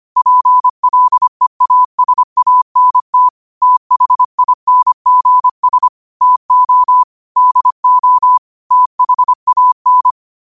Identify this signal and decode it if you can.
Morse code, heard as a keyed tone, about 25 wpm, 1000 Hz. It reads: PLEASANT THINGS TO DO THAN
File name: qotdfiles / 25wpm